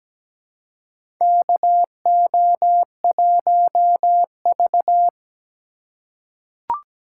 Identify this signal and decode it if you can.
Morse code, heard as a keyed tone, about 17 wpm, 700 Hz. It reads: KO1V
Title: USA Call Signs - 3x Character Spacing - 17wpm